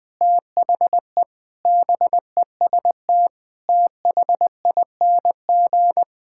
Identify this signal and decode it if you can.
Morse code, heard as a keyed tone, about 21 wpm, 700 Hz. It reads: THE BEST THING
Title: Sets of 8 Words - 20wpm